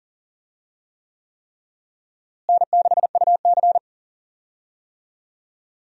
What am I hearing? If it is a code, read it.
Morse code, heard as a keyed tone, about 40 wpm, 700 Hz. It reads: N6UC
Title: USA Call Signs - 4 Characters - Repeated 3x - 40wpm